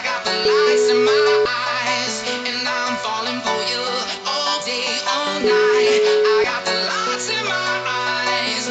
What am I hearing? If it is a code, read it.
Ringback tone (Europe, 425 Hz), with noise.